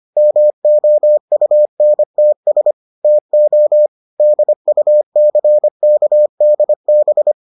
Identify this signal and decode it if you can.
Morse code, heard as a keyed tone, about 25 wpm, 600 Hz. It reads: MOUNTS TO DUCKDB